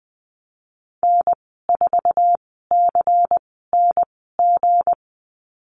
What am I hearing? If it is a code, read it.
Morse code, heard as a keyed tone, about 20 wpm, spaced as for 14 wpm, 700 Hz. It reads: N4CNG
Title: 193 Lesson 31g - Letter v - Call Signs 20wpm